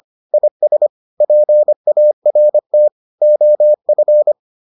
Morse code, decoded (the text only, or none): IS PART OF